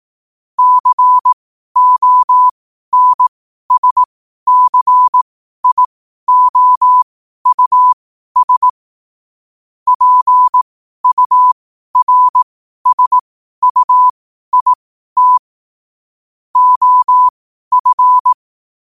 Morse code, heard as a keyed tone, about 18 wpm, spaced as for 12 wpm, 1000 Hz. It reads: CONSCIOUS PURSUIT OF
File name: qotdfiles / 12wpm